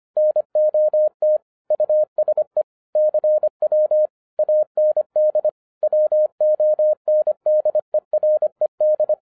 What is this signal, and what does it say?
Morse code, heard as a keyed tone, about 26 wpm, 610 Hz. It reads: NOT USE CW AND WONDERED